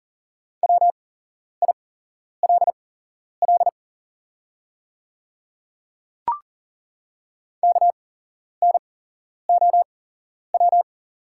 Morse code, heard as a keyed tone, about 40 wpm, spaced as for 11 wpm, 690 Hz. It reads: WILL KNOW
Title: 8x Character Spacing - Top 100 Words - 40wpm